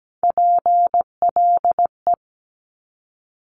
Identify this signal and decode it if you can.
Morse code, heard as a keyed tone, about 17 wpm, 700 Hz. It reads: PLE
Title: Top Three Letter Combinations in Words 17wpm